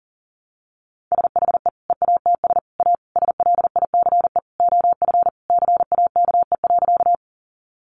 Morse code, heard as a keyed tone, about 40 wpm, 700 Hz. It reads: SHE EATS A SLICE OF CAKE.
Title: Easy Phrases 40wpm